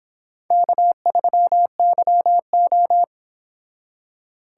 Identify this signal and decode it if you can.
Morse code, heard as a keyed tone, about 25 wpm, 700 Hz. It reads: K3YO